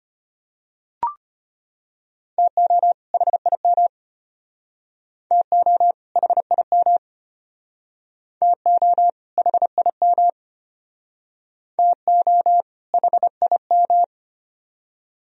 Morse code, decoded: TO HIM TO HIM TO HIM TO HIM

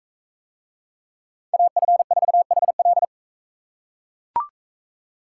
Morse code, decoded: AF4HL

